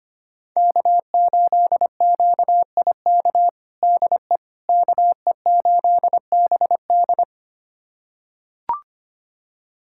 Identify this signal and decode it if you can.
Morse code, heard as a keyed tone, about 25 wpm, 700 Hz. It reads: K8QIK DE KE8BD
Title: USA Callsign DE Callsign - 5 Characters - 25wpm